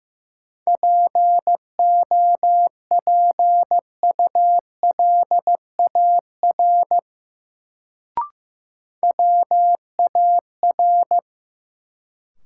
Morse code, heard as a keyed tone, about 15 wpm, 700 Hz. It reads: POPULAR WAR
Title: Sets of 5 Words - New Word Drill - 15wpm